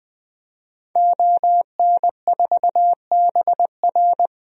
Morse code, heard as a keyed tone, about 20 wpm, 700 Hz. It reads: ON4BR
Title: DXCC-100 Call Signs - 5 Characters - 20wpm